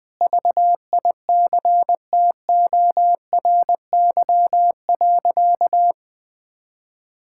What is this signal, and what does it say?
Morse code, heard as a keyed tone, about 21 wpm, 700 Hz. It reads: VICTORY.
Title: Sentences with 9-Word Phrases - 20wpm